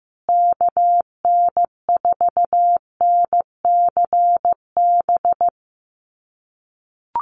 Morse code, heard as a keyed tone, about 15 wpm, 700 Hz. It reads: KN4NCB
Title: USA Call Signs Repeated 3x 15wpm